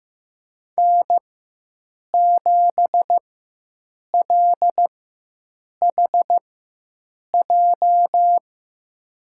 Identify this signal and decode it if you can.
Morse code, heard as a keyed tone, about 15 wpm, spaced as for 7 wpm, 700 Hz. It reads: N7LHJ
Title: USA Call Signs - 5 Characters - 4x Character Spacing - 15wpm